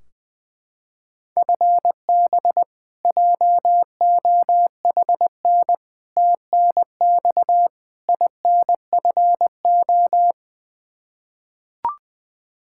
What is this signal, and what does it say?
Morse code, heard as a keyed tone, about 20 wpm, 700 Hz. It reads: FB JOHN TNX INFO
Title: Practice QSO - Exchange 3 - More Info - 20wpm